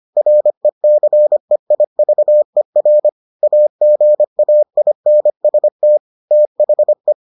Morse code, heard as a keyed tone, about 25 wpm, 600 Hz. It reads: RECEIVER AGAINST THE